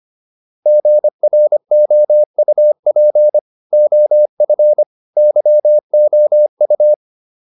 Morse code, heard as a keyed tone, about 25 wpm, 600 Hz. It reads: GROUP OF YOU
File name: PR405 25